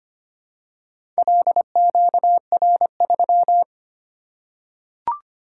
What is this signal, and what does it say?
Morse code, heard as a keyed tone, about 25 wpm, 700 Hz. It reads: LQR3
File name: practice / 25wpm / Four Letter-Number Combinations - 25wpm